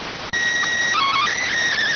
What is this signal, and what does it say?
SSTV (slow-scan television), PD 290.